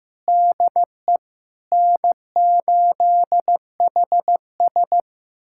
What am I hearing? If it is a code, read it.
Morse code, heard as a keyed tone, about 15 wpm, 700 Hz. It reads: DE N8HS
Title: USA Callsign DE Callsign - 4 Characters - 15wpm